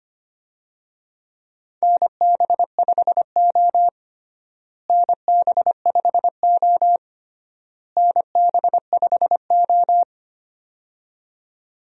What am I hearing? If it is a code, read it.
Morse code, heard as a keyed tone, about 25 wpm, 700 Hz. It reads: NB5O NB5O NB5O
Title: USA Call Signs - 4 Characters - Repeated 3x - 25wpm